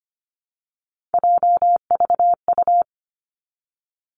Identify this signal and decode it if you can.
Morse code, heard as a keyed tone, about 25 wpm, 700 Hz. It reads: JVU
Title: Three Challenging Letter-Number Combinations 25wpm